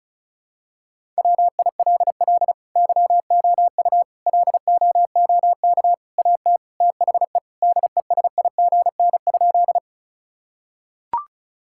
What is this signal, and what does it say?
Morse code, heard as a keyed tone, about 36 wpm, 700 Hz. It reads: WILL YOU LOOK AT THE DESIGN?
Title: Sentences from Top 1000 Words - Review Part 5 - 35wpm